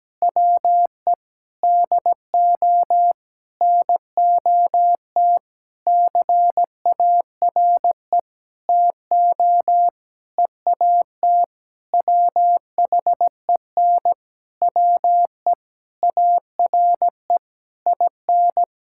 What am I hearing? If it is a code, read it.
Morse code, heard as a keyed tone, about 17 wpm, 700 Hz. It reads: WE DO NOT CARE TO EAT WHEN WE ARE IN
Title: Sentences from Top 1600 Words 17wpm